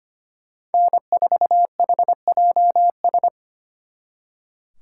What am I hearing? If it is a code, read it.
Morse code, heard as a keyed tone, about 30 wpm, 700 Hz. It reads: N4HJS
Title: Speed Racing - USA 1x3 Call Signs 25wpm